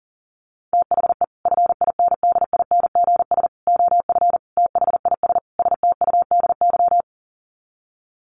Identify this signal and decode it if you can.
Morse code, heard as a keyed tone, about 40 wpm, 700 Hz. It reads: THE FINDINGS OF THIS STUDY
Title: Sets of 5 Words Encore2 40wpm